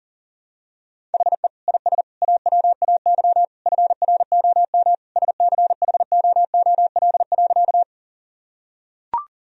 Morse code, heard as a keyed tone, about 37 wpm, 700 Hz. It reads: HE IS AWAY FROM SCHOOL.
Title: Sentences from Top 1000 Words - Review Part 1 - 40wpm